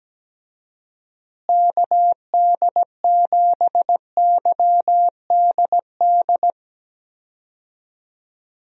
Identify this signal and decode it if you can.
Morse code, heard as a keyed tone, about 17 wpm, 700 Hz. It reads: KD7YDD
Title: USA Call Signs - 6 Characters - Repeated 2x - 17wpm